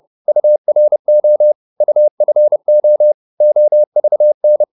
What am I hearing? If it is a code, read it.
Morse code, heard as a keyed tone, about 30 wpm, 600 Hz. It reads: URO UFO OVN
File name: Korte ord DK 30wpm